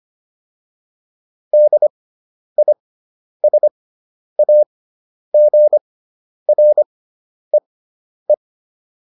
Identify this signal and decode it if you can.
Morse code, heard as a keyed tone, about 25 wpm, spaced as for 10 wpm, 600 Hz. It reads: DISAGREE